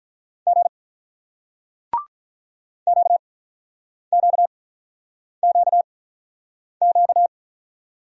Morse code, heard as a keyed tone, about 47 wpm, spaced as for 9 wpm, 700 Hz. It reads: G QQQQ